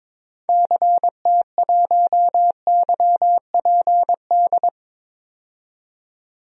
Morse code, heard as a keyed tone, about 22 wpm, 700 Hz. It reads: CT1YPD
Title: DXCC-100 Call Signs - 6 Characters - 22wpm